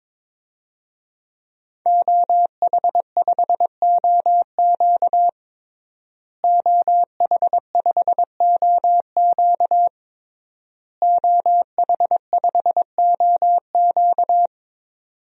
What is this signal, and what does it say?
Morse code, heard as a keyed tone, about 21 wpm, 700 Hz. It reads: OH5OQ OH5OQ OH5OQ
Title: DXCC-100 Call Signs - 5 Characters - Repeated 3x - 22wpm